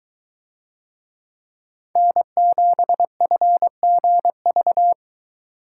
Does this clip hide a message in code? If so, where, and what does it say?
Yes: Morse code from 2.0 to 4.9 s, heard as a keyed tone, about 23 wpm, 700 Hz. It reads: N7FGV